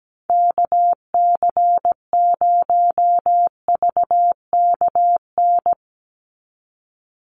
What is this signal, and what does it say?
Morse code, heard as a keyed tone, about 17 wpm, 700 Hz. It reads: KC0VKN
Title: Call Signs Repeated 2x - 17wpm